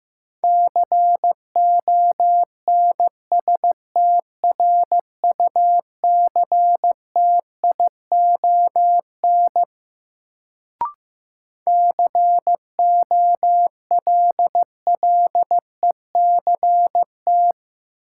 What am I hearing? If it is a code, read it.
Morse code, heard as a keyed tone, about 15 wpm, 700 Hz. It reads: CONSTRUCTION COLLECT
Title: Sentences from Top 1600 Words - New Word Drill - 15wpm